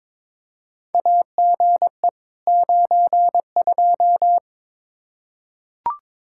Morse code, heard as a keyed tone, about 22 wpm, 700 Hz. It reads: AGE 92
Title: Practice QSO - Exchange 3 - More Info - 22wpm